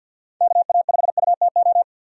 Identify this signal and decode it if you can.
Morse code, heard as a keyed tone, about 48 wpm, 700 Hz. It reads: KA5UTO
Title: USA Call Signs - 6 Characters - 50wpm